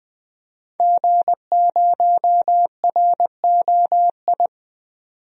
Morse code, heard as a keyed tone, about 20 wpm, 700 Hz. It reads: G0ROI
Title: DXCC-100 Call Signs - 5 Characters - 20wpm